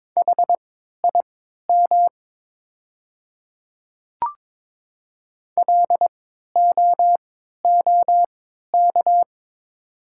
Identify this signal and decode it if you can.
Morse code, heard as a keyed tone, about 22 wpm, spaced as for 12 wpm, 690 Hz. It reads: HIM LOOK
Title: 3x Character Spacing - Top 100 Words - 22wpm